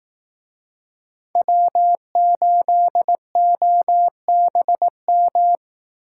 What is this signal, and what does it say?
Morse code, heard as a keyed tone, about 18 wpm, 700 Hz. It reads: W8OBM
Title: Speed Racing - USA 1x3 Call Signs 15wpm